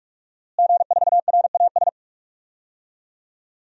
Morse code, heard as a keyed tone, about 46 wpm, 700 Hz. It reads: G4RAS
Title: DXCC-100 Call Signs - 5 Characters - Repeated 2x - 45wpm